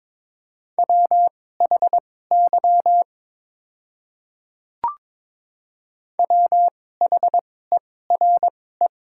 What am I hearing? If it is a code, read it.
Morse code, heard as a keyed tone, about 22 wpm, spaced as for 16 wpm, 700 Hz. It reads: WHY WHERE
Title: 2x Character Spacing - Top 200 Words - 22wpm